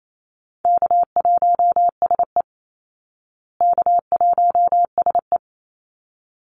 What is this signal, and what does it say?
Morse code, heard as a keyed tone, about 28 wpm, 700 Hz. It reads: K1SE K1SE
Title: Easy Call Signs Repeated 2x - 28wpm